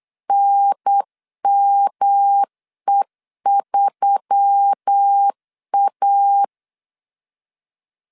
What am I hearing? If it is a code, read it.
Morse code, heard as a keyed tone, about 8 wpm, 800 Hz. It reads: NME3A
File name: LEZIONI / 6 PKX0 / 6 PKX 8wpm 3041